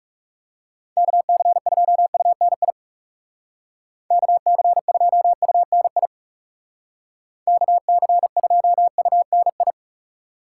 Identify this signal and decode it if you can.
Morse code, heard as a keyed tone, about 38 wpm, 700 Hz. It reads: KC2UNI KC2UNI KC2UNI